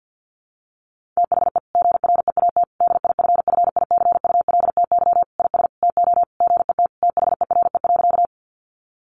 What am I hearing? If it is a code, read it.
Morse code, heard as a keyed tone, about 50 wpm, 700 Hz. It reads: THE GREAT DIFFICULTY IS TO GET THERE.
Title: Sentences from Top 1500 Words 50wpm